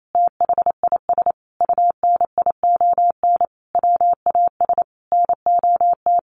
Morse code, heard as a keyed tone, about 28 wpm, 700 Hz. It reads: THIS UNION WAS NOT
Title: Sentences from Top 1400 Words 28wpm